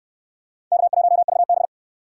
Morse code, heard as a keyed tone, about 63 wpm, 700 Hz. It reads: K1VB